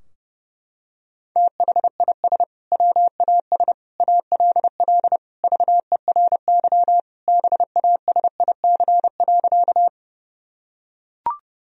Morse code, heard as a keyed tone, about 30 wpm, 700 Hz. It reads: THIS WAS ALL VERY BASIC.
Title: Sentences from Top 1000 Words - Review Part 3 - 30wpm